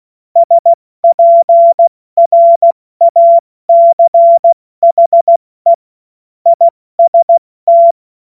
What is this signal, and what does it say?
Morse code, heard as a keyed tone, about 16 wpm, 680 Hz. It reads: SPRACHE IST
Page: cw-witze